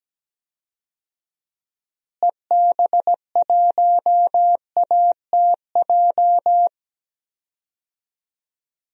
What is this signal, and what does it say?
Morse code, heard as a keyed tone, about 17 wpm, 700 Hz. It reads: EB1ATJ
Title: DXCC-100 Call Signs - 6 Characters - Repeated 2x - 17wpm